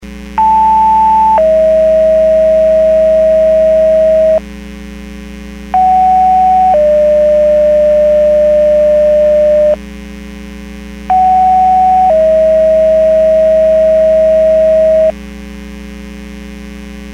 Two-tone page, heard as a one-second tone, then a three-second tone, about 880 then 640 Hz, about 750 then 600 Hz, about 750 then 640 Hz. It is heard from 0.4 to 15.1 s.
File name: Portage 154.385 Paging